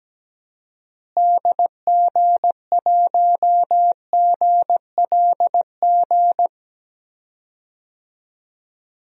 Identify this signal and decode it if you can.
Morse code, heard as a keyed tone, about 17 wpm, 700 Hz. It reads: DG1GLG
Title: DXCC-100 Call Signs - 6 Characters - Repeated 2x - 17wpm